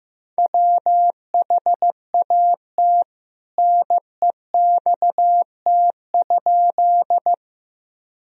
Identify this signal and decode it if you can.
Morse code, heard as a keyed tone, about 15 wpm, 700 Hz. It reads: WHAT NEXT?
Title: Sentences from Top 1000 Words - Review Part 1 - 15wpm